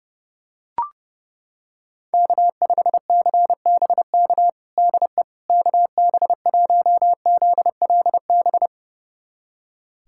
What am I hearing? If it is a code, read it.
Morse code, heard as a keyed tone, about 32 wpm, 700 Hz. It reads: K5CBK DE KB1ZLB
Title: USA Callsign DE Callsign 30wpm